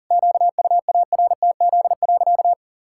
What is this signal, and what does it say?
Morse code, heard as a keyed tone, about 40 wpm, 700 Hz. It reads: QUARTZ.
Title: Pangram Challenge v2 - 40wpm